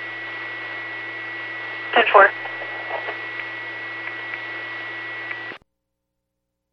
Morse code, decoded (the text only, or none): none